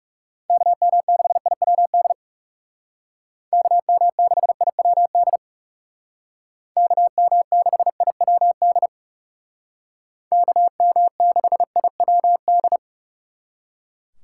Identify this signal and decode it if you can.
Morse code, heard as a keyed tone, about 39 wpm, 700 Hz. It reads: KM6IWD KM6IWD KM6IWD KM6IWD